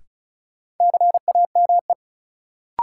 Morse code, heard as a keyed tone, about 36 wpm, 700 Hz. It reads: CAME
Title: Sentences from Top 200 Words - New Word Drill - 35wpm